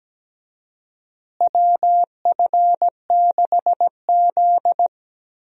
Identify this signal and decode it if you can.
Morse code, heard as a keyed tone, about 17 wpm, 700 Hz. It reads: WF6Z
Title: USA Call Signs - 4 Characters - Repeated 3x - 17wpm